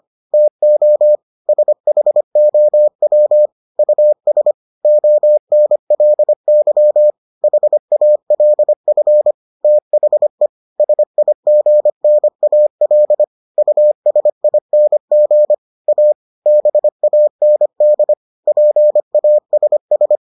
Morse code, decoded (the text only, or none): TO SHOW US ONLY HALF THE SIGNAL USING A BAND PASS